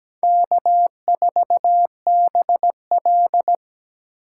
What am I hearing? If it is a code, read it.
Morse code, heard as a keyed tone, about 20 wpm, 700 Hz. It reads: K4BL
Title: Speed Racing - USA 1x2 Call Signs 17wpm